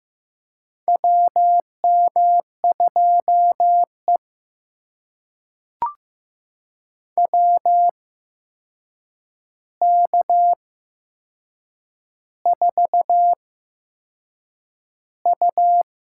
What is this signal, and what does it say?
Morse code, heard as a keyed tone, about 15 wpm, spaced as for 6 wpm, 700 Hz. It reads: WM2E WK4U
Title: USA Call Signs - 4 Characters - 8x Character Spacing - 15wpm